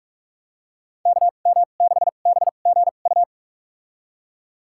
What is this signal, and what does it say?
Morse code, heard as a keyed tone, about 46 wpm, spaced as for 33 wpm, 700 Hz. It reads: KM6BGU
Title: USA Call Signs - 6 Characters - 2x Character Spacing - 45wpm